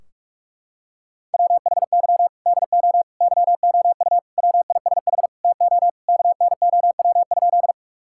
Morse code, heard as a keyed tone, about 46 wpm, 700 Hz. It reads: WHY DO YOU WISH TO KNOW?
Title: Sentences from Top 1600 Words 45wpm